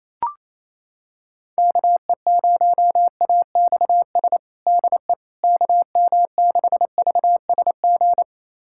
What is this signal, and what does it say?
Morse code, heard as a keyed tone, about 27 wpm, 700 Hz. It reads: KE0AXS DE KM6VSG
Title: USA Callsign DE Callsign 28wpm